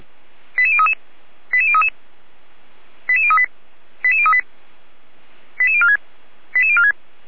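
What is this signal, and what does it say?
Five-tone selective call (ZVEI), heard as quick runs of five tones, tones of about 70 ms. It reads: 88033 88033 88038 88038 88046 88046
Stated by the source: ZVEI five-tone selective call 88159, received by SDR.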